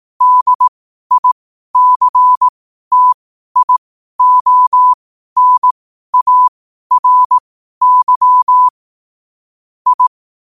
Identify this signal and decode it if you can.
Morse code, heard as a keyed tone, about 18 wpm, spaced as for 12 wpm, 1000 Hz. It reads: DICTIONARY I
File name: qotdfiles / 12wpm